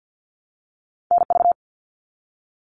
Morse code, heard as a keyed tone, about 50 wpm, 700 Hz. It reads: NV
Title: US States Repeated 3x 50wpm